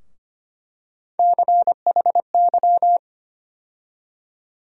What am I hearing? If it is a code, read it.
Morse code, heard as a keyed tone, about 25 wpm, 700 Hz. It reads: CHY
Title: USA Call Sign Postfixes - 3 Letters 25wpm